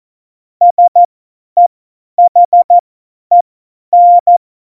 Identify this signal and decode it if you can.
Morse code, heard as a keyed tone, about 14 wpm, spaced as for 10 wpm, 700 Hz. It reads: SEHEN
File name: Morsekurs / Witze / Witz 15. Dezember 2020, 50 F70